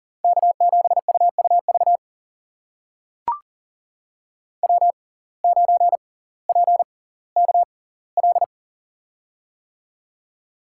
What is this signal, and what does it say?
Morse code, heard as a keyed tone, about 38 wpm, spaced as for 20 wpm, 700 Hz. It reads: K7UUV W9PKL
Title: USA Call Signs - 6x Character Spacing - 40wpm